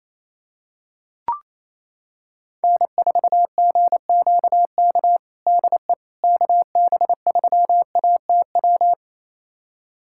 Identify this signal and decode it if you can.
Morse code, heard as a keyed tone, about 28 wpm, 700 Hz. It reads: N4GQK DE KB3ATW
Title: USA Callsign DE Callsign 28wpm